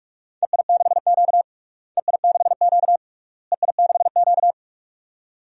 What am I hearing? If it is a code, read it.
Morse code, heard as a keyed tone, about 46 wpm, 700 Hz. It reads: EI6Q EI6Q EI6Q
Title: DXCC-100 Call Signs - 4 Characters - Repeated 3x - 45wpm